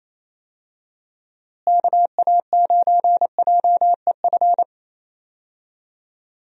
Morse code, heard as a keyed tone, about 28 wpm, 700 Hz. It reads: KA9JEF
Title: USA Call Signs - 6 Characters - Repeated 2x - 28wpm